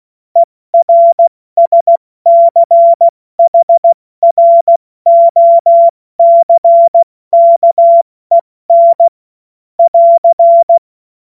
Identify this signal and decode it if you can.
Morse code, heard as a keyed tone, about 16 wpm, 680 Hz. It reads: ERSCHROCKEN <AR>
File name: Morsekurs / Witze / Witz 19 Dezember 2023 70 F80